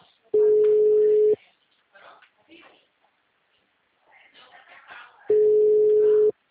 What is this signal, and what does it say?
Ringback tone (Europe, 425 Hz), with speech.
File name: out- -2011- - - .61